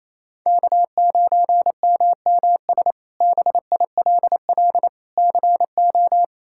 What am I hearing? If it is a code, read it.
Morse code, heard as a keyed tone, about 28 wpm, 700 Hz. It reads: K9MMS BILL CO
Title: K1USN SST QSO Practice - 28wpm